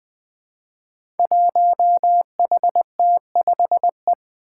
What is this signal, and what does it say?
Morse code, heard as a keyed tone, about 20 wpm, 700 Hz. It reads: 1HT5E
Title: Five Challenging Letter-Number Combinations 20wpm